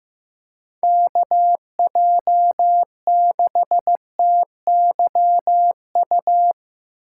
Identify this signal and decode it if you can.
Morse code, heard as a keyed tone, about 17 wpm, 700 Hz. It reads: KJ6TYU
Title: Speed Racing - USA Call Signs 15wpm